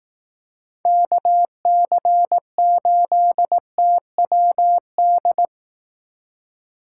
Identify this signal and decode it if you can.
Morse code, heard as a keyed tone, about 18 wpm, 700 Hz. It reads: KC8TWD
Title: Speed Racing - USA 2x3 Call Signs 15wpm